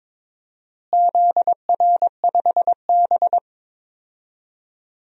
Morse code, heard as a keyed tone, about 22 wpm, 700 Hz. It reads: ZR5B
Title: DXCC-100 Call Signs - 4 Characters - Repeated 3x - 22wpm